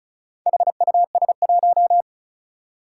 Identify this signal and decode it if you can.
Morse code, heard as a keyed tone, about 36 wpm, 700 Hz. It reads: HUS1
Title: Four Challenging Letter-Number Combinations 35wpm